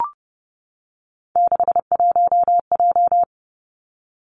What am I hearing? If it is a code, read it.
Morse code, heard as a keyed tone, about 30 wpm, 700 Hz. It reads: 61J
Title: Three Challenging Letter-Number Combinations 30wpm